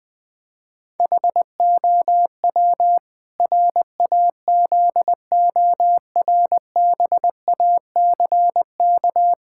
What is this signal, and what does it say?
Morse code, heard as a keyed tone, about 20 wpm, 700 Hz. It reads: HOW RAZORBACK
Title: Pangram Challenge v2 - 20wpm